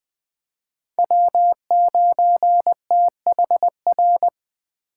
Morse code, heard as a keyed tone, about 21 wpm, 700 Hz. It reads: W9THR